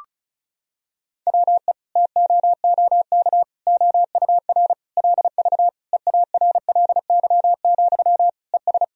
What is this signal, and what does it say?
Morse code, heard as a keyed tone, about 36 wpm, 700 Hz. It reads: WE TOOK OUR LV EARLY, ES
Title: Sentences from Top 400 Words and Abbreviations 35wpm